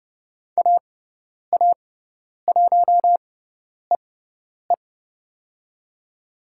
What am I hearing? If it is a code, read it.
Morse code, heard as a keyed tone, about 30 wpm, spaced as for 10 wpm, 700 Hz. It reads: AA1EE
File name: Lesson 18g - Letter m - Call Signs 30wpm